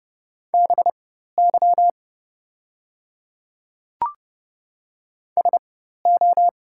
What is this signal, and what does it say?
Morse code, heard as a keyed tone, about 30 wpm, spaced as for 14 wpm, 700 Hz. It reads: BY SO
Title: 4x Character Spacing - Top 100 Words - 30wpm